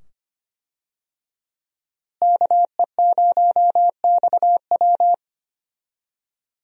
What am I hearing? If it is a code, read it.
Morse code, heard as a keyed tone, about 25 wpm, 700 Hz. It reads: KE0XW
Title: USA Call Signs - 5 Characters - Repeated 2x - 25wpm